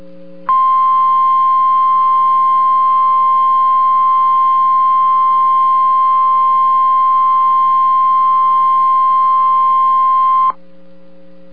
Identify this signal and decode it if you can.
Emergency Alert System (SAME) tones (the 1050 Hz attention tone). It is heard from 0.5 to 10.6 s.